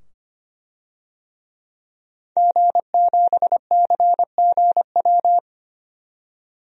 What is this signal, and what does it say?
Morse code, heard as a keyed tone, about 25 wpm, 700 Hz. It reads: G7CGW
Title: DXCC-100 Call Signs - 5 Characters - 25wpm